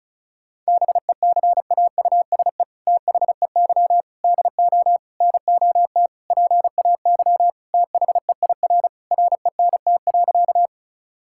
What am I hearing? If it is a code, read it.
Morse code, heard as a keyed tone, about 36 wpm, 700 Hz. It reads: BECAUSE THEY DO NOT PAY THEIR RENT.
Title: Sentences from Top 1000 Words - Review Part 5 - 35wpm